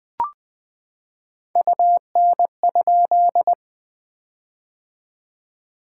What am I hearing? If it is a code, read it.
Morse code, heard as a keyed tone, about 20 wpm, 700 Hz. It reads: UN?
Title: POTA QSO Practice - 20wpm